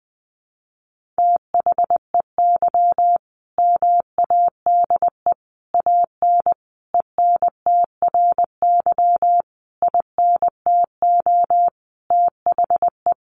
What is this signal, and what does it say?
Morse code, heard as a keyed tone, about 20 wpm, 700 Hz. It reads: THEY MADE AN ENTRY INTO THE
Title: Sentences from Top 1500 Words 20wpm